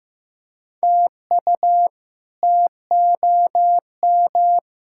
Morse code, heard as a keyed tone, about 15 wpm, 700 Hz. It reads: TU TOM
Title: K1USN SST QSO Practice - 15wpm